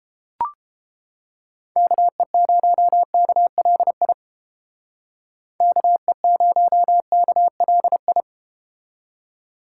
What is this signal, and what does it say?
Morse code, heard as a keyed tone, about 31 wpm, 700 Hz. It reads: KE0KLI KE0KLI